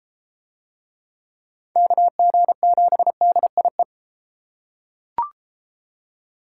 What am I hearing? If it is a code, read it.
Morse code, heard as a keyed tone, about 30 wpm, 700 Hz. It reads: KG7DIE